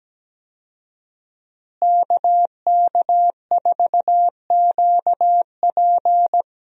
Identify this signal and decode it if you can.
Morse code, heard as a keyed tone, about 17 wpm, 700 Hz. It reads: KK4QP
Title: USA Call Signs - 5 Characters - Repeated 3x - 17wpm